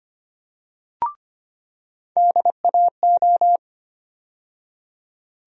Morse code, heard as a keyed tone, about 25 wpm, 700 Hz. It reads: DAO
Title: USA Call Sign Postfixes - 3 Letters 25wpm